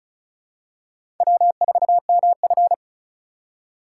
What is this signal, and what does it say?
Morse code, heard as a keyed tone, about 36 wpm, 700 Hz. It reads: W4MF
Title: USA Call Signs - 4 Characters - 4x Character Spacing - 35wpm